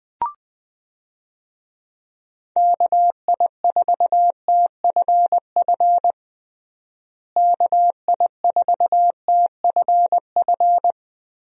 Morse code, heard as a keyed tone, about 20 wpm, 700 Hz. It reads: KI4TFF KI4TFF
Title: USA Call Signs - 6 Characters - Repeated 3x - 20wpm